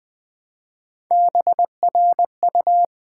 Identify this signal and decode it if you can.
Morse code, heard as a keyed tone, about 20 wpm, 700 Hz. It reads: BRU
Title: Build A Word - 1 Letter At A Time - 5-Letter Words - 20wpm